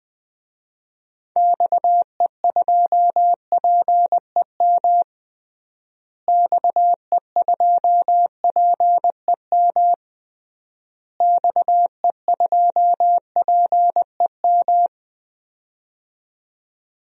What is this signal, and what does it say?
Morse code, heard as a keyed tone, about 20 wpm, 700 Hz. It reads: XE2PEM XE2PEM XE2PEM
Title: DXCC-100 Call Signs - 6 Characters - Repeated 3x - 20wpm